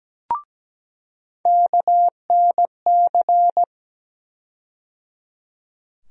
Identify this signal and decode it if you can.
Morse code, heard as a keyed tone, about 17 wpm, 700 Hz. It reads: KNC
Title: USA Call Sign Postfixes - 3 Letters 17wpm